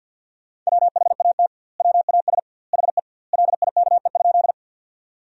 Morse code, heard as a keyed tone, about 50 wpm, 700 Hz. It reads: WHAT WAS HE LIKE?
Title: 2x Word-Spacing - Sentences from Top 100 Words - 50wpm